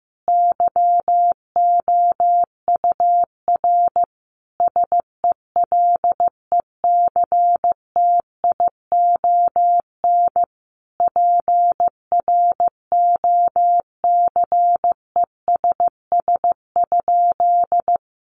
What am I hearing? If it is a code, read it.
Morse code, heard as a keyed tone, about 15 wpm, 700 Hz. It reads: YOUR SELECTION PROCESS?
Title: Sentences from Top 1400 Words 15wpm